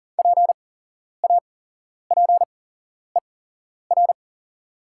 Morse code, heard as a keyed tone, about 40 wpm, spaced as for 11 wpm, 700 Hz. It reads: PAPER
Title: 8x Character Spacing - Top 300 Words - 40wpm